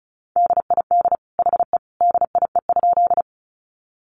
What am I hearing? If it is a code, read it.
Morse code, heard as a keyed tone, about 34 wpm, 700 Hz. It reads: DID HE DIE?
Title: Sentences from Top 1300 Words 35wpm